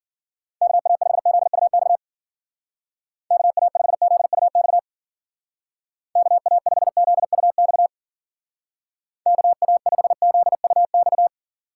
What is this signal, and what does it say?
Morse code, heard as a keyed tone, about 48 wpm, 700 Hz. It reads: KA5ZUX KA5ZUX KA5ZUX KA5ZUX
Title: Speed Racing - USA 2x3 Call Signs 40wpm